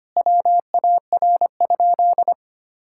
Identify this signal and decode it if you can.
Morse code, heard as a keyed tone, about 25 wpm, 700 Hz. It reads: WAR?
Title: Sentences from Top 1000 Words - Review Part 5 - 25wpm